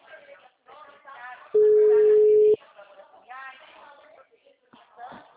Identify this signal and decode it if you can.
Ringback tone (Europe, 425 Hz), with speech.
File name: out- -2012- - - .39970